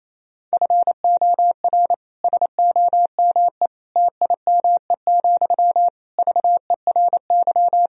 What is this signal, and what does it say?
Morse code, heard as a keyed tone, about 28 wpm, 700 Hz. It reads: FOR SOME TIME, VERY
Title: Sentences from Top 1000 Words - Review Part 4 - 28wpm